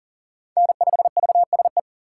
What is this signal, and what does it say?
Morse code, heard as a keyed tone, about 38 wpm, 700 Hz. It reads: N5VSE
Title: USA Call Signs - 5 Characters - 6x Character Spacing - 40wpm